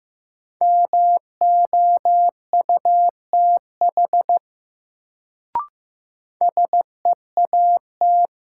Morse code, heard as a keyed tone, about 15 wpm, 700 Hz. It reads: MOUTH SEAT
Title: Sentences from Top 1500 Words - New Word Drill - 15wpm